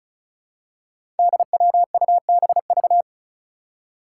Morse code, heard as a keyed tone, about 36 wpm, 700 Hz. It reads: DWUBV
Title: Five Challenging Letter-Number Combinations 35wpm